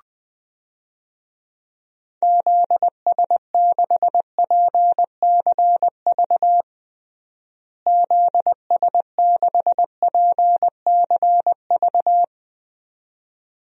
Morse code, heard as a keyed tone, about 20 wpm, 700 Hz. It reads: ZS6PCV ZS6PCV
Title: DXCC-100 Call Signs - 6 Characters - Repeated 2x - 20wpm